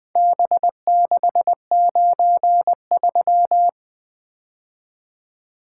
Morse code, heard as a keyed tone, about 20 wpm, 700 Hz. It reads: B693
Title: Four Letter-Number Combinations - 20wpm